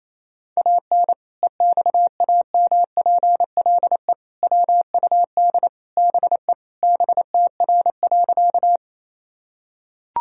Morse code, heard as a keyed tone, about 28 wpm, 700 Hz. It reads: AN EXAMPLE WUD BE BTR.
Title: Sentences from Top 400 Words and Abbreviations 28wpm